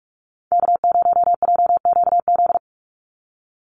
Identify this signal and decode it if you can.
Morse code, heard as a keyed tone, about 44 wpm, 700 Hz. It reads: K0JQZ
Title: NA SOTA Activator Callsigns 45wpm